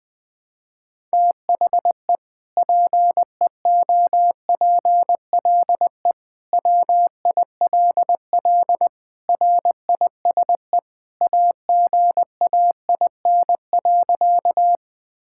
Morse code, heard as a keyed tone, about 20 wpm, 700 Hz. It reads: THE PEOPLE WILL RISE AGAIN.
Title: Sentences from Top 1000 Words - Review Part 6 - 20wpm